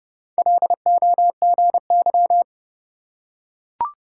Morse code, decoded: LOGY

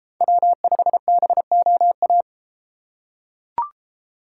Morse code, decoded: W5BOA

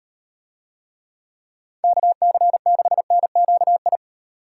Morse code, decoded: KC6NQI

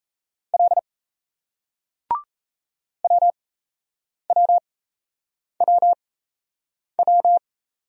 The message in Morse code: L WWWW

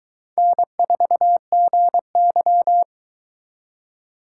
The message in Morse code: N4GY

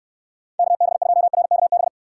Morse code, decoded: KB2RKB

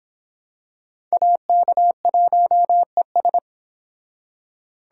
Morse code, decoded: AK1ES

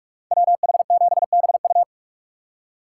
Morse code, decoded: WH7BU

